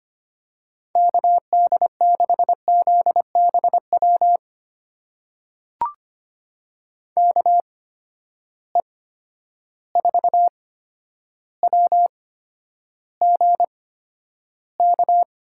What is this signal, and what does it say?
Morse code, heard as a keyed tone, about 25 wpm, spaced as for 11 wpm, 700 Hz. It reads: KD6ZBW KE4WGK
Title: USA Call Signs - 6 Characters - 8x Character Spacing - 25wpm